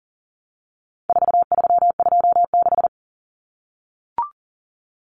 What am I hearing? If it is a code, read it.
Morse code, heard as a keyed tone, about 40 wpm, 700 Hz. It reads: 4326
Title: Common Cut Numbers - 4-Digit 40wpm